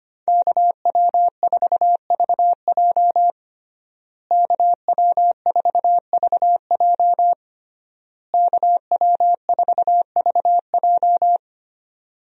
Morse code, heard as a keyed tone, about 25 wpm, 700 Hz. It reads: KW4VJ KW4VJ KW4VJ
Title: USA Call Signs - 5 Characters - Repeated 3x - 25wpm